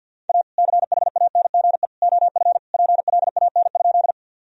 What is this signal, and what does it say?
Morse code, heard as a keyed tone, about 50 wpm, 700 Hz. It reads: A CHANGE OF PLAN?
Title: Sentences from Top 1000 Words - Review Part 2 - 50wpm